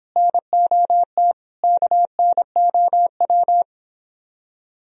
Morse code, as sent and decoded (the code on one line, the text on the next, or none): -. --- - / -.- -. --- .--
NOT KNOW